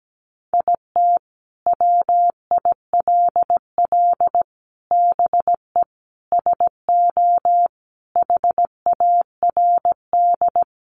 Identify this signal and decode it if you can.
Morse code, heard as a keyed tone, about 17 wpm, 700 Hz. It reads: IT WILL BE SO HARD
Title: Sentences from Top 1500 Words 17wpm